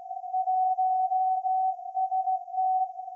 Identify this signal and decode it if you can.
Morse code, heard as a keyed tone, about 16 wpm, 750 Hz. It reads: 1ST
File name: iz7slz / VLF / SAQ 30-06-2024 0900